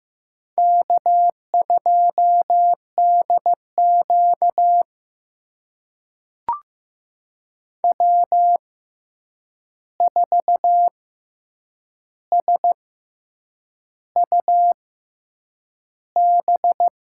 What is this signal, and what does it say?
Morse code, heard as a keyed tone, about 15 wpm, spaced as for 7 wpm, 700 Hz. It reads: K2DQ W4SUB
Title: USA Call Signs - 6x Character Spacing - 15wpm